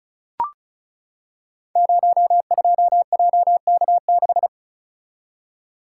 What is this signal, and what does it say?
Morse code, heard as a keyed tone, about 36 wpm, 690 Hz. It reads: 02JK6
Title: Code Groups - 5 Letter:Numbers - 35wpm